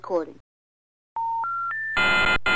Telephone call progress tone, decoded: special information tone